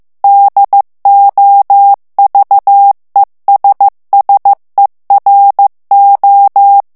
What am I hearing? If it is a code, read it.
Morse code, heard as a keyed tone, about 15 wpm, 800 Hz. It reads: DOVESSERO
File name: morse mp3 / ITALIEN texte clair 2 vitesse 14 mots minute